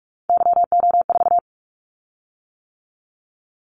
Morse code, heard as a keyed tone, about 46 wpm, 700 Hz. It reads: YO4
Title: Callsign Prefixes for Top 25 DXCC - 45wpm